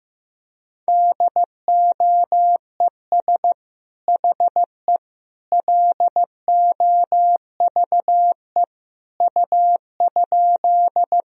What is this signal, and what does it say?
Morse code, heard as a keyed tone, about 15 wpm, 700 Hz. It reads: DOES HE LOVE U?
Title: Sentences from Top 400 Words and Abbreviations 15wpm